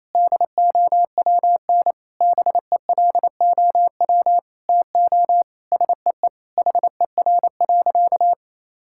Morse code, heard as a keyed tone, about 28 wpm, 700 Hz. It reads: DOWN BELOW TO SEE HER.
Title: Sentences from Top 1000 Words - Review Part 5 - 28wpm